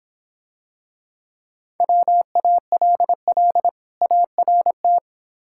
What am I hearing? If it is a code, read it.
Morse code, heard as a keyed tone, about 25 wpm, 700 Hz. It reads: WALL ART